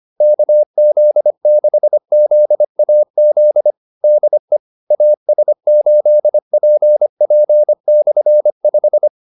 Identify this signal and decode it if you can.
Morse code, heard as a keyed tone, about 25 wpm, 600 Hz. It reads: KZ6ZAZ DE AS8PP/5